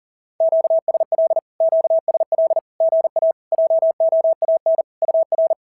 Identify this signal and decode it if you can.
Morse code, heard as a keyed tone, about 37 wpm, 690 Hz. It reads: QSL QSL GA JOAN UR
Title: SOTA QSO Practice 40wpm